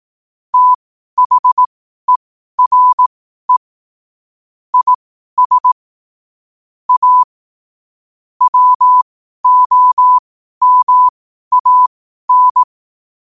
Morse code, heard as a keyed tone, about 18 wpm, spaced as for 12 wpm, 1000 Hz. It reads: THERE IS A WOMAN
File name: qotdfiles / 12wpm